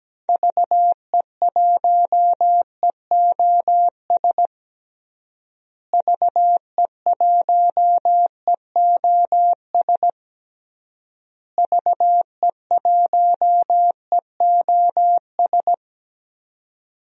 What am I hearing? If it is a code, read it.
Morse code, heard as a keyed tone, about 17 wpm, 700 Hz. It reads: VE1EOS VE1EOS VE1EOS
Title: DXCC-100 Call Signs - 6 Characters - Repeated 3x - 17wpm